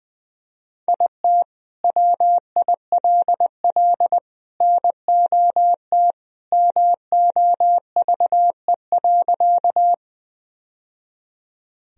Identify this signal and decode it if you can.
Morse code, heard as a keyed tone, about 20 wpm, 700 Hz. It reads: IT WILL NOT MOVE.
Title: Sentences from Top 1000 Words - Review Part 1 - 20wpm